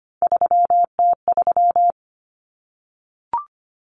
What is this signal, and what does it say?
Morse code, heard as a keyed tone, about 25 wpm, 700 Hz. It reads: 3T3
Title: Common Cut Numbers - 3-Digit 25wpm